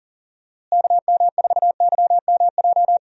Morse code, heard as a keyed tone, about 40 wpm, 700 Hz. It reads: KM4YMJ
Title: USA Call Signs - 6 Characters - Repeated 2x - 40wpm